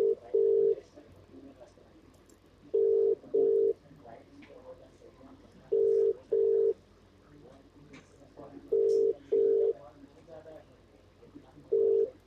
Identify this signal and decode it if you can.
Ringback tone (UK), 400 + 450 Hz, with speech.